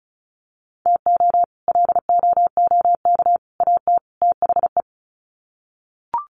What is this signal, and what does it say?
Morse code, heard as a keyed tone, about 34 wpm, 700 Hz. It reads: TO LOOK AT THE
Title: Sets of 4 Words 35wpm